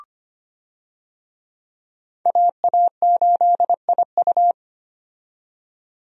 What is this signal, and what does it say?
Morse code, heard as a keyed tone, about 25 wpm, 700 Hz. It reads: AA8IU
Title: USA Call Signs - 5 Characters - 25wpm